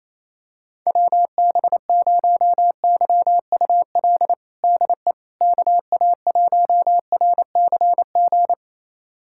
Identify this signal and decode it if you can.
Morse code, heard as a keyed tone, about 28 wpm, 700 Hz. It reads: WB0YUL DE KA1RCG
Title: USA Callsign DE Callsign 28wpm